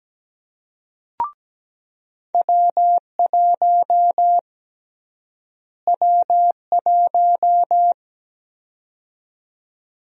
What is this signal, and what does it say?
Morse code, heard as a keyed tone, about 17 wpm, 700 Hz. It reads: W1 W1
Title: USA Call Sign Prefixes - Repeated 2x - 17wpm